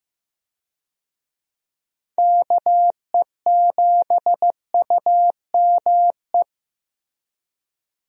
Morse code, heard as a keyed tone, about 15 wpm, 700 Hz. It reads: KE7UME
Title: USA Call Signs - 6 Characters - Repeated 2x - 15wpm